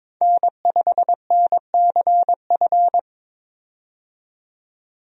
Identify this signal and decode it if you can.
Morse code, heard as a keyed tone, about 22 wpm, 700 Hz. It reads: N5NCF
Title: USA Call Signs - 5 Characters - 22wpm